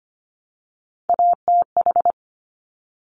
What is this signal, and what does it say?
Morse code, heard as a keyed tone, about 25 wpm, 700 Hz. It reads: ATH
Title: Top Three Letter Combinations in Words 25wpm